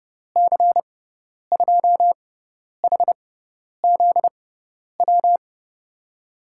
Morse code, heard as a keyed tone, about 30 wpm, spaced as for 10 wpm, 700 Hz. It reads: C2HZW
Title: USA Call Signs - 6 Characters - 6x Character Spacing - 30wpm